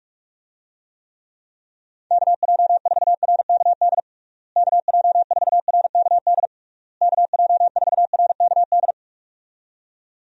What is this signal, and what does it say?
Morse code, heard as a keyed tone, about 46 wpm, 700 Hz. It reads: KJ4RKD KJ4RKD KJ4RKD
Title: USA Call Signs - 6 Characters - Repeated 3x - 45wpm